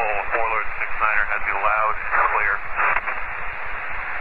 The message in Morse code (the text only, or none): none